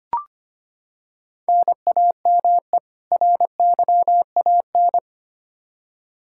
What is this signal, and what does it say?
Morse code, heard as a keyed tone, about 25 wpm, 700 Hz. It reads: NAME RYAN
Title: K1USN SST QSO Practice 25wpm